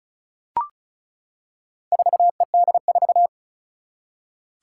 Morse code, heard as a keyed tone, about 36 wpm, 690 Hz. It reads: 4ED4
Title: Four Challenging Letter-Number Combinations 35wpm